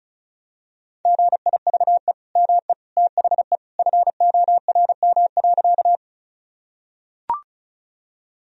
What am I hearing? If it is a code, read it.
Morse code, heard as a keyed tone, about 36 wpm, 700 Hz. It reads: GIVE ME THE FORM.
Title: Sentences from Top 1000 Words - Review Part 1 - 35wpm